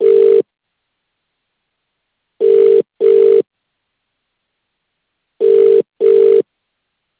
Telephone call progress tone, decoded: ringback tone